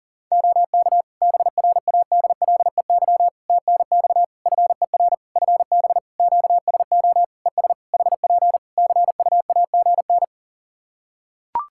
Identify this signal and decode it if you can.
Morse code, heard as a keyed tone, about 40 wpm, 700 Hz. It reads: OK BRADLEY TNX FER FB QSO ES HP CUAGN
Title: Practice QSO - Exchange 4 - Ending - 40wpm